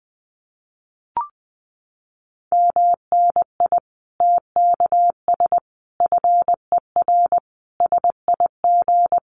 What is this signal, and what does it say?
Morse code, heard as a keyed tone, about 20 wpm, 700 Hz. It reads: MNI TKS FER SIG
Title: Common QSO Phrases 20wpm